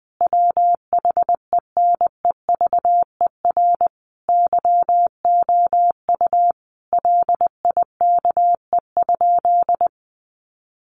Morse code, heard as a keyed tone, about 20 wpm, 700 Hz. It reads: WHENEVER YOU LIKE?
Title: Sentences from Top 1200 Words 20wpm